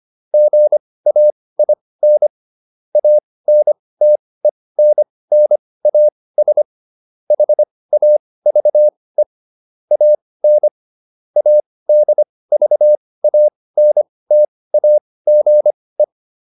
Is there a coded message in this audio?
Yes: Morse code from 0.3 to 16.0 s, heard as a keyed tone, about 25 wpm, spaced as for 18 wpm, 600 Hz. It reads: GAIN ANTENNAS HAVE AN ADVANTAGE